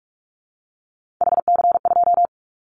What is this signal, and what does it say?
Morse code, heard as a keyed tone, about 46 wpm, 700 Hz. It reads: HC2
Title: Callsign Prefixes for Top 100 DXCC - 45wpm